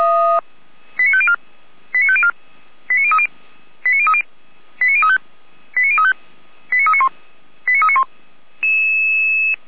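Five-tone selective call (ZVEI), heard as quick runs of five tones, tones of about 70 ms. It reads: 89504 89504 89030 89030 89035 89035 89391 89391
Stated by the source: ZVEI five-tone selective call received on an SDR.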